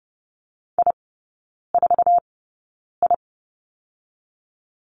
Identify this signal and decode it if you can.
Morse code, heard as a keyed tone, about 30 wpm, spaced as for 9 wpm, 700 Hz. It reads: I4I
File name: Lesson 16e - Letter c - Three Characters 30wpm